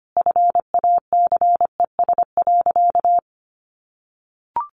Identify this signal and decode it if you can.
Morse code, heard as a keyed tone, about 25 wpm, 700 Hz. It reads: FACES.
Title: A Sound Of Thunder 25wpm